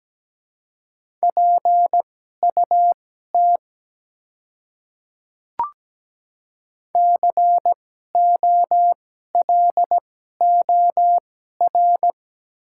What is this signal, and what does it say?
Morse code, heard as a keyed tone, about 17 wpm, spaced as for 12 wpm, 690 Hz. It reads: PUT COLOR
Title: 2x Character Spacing - Top 300 Words - 17wpm